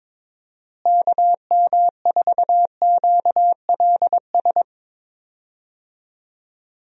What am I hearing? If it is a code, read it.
Morse code, heard as a keyed tone, about 22 wpm, 700 Hz. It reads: KM4QLS
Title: USA Call Signs - 6 Characters - Repeated 2x - 22wpm